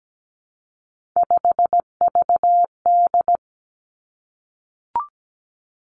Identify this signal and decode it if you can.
Morse code, heard as a keyed tone, about 17 wpm, 700 Hz. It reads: 5VD
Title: Three Challenging Letter-Number Combinations 17wpm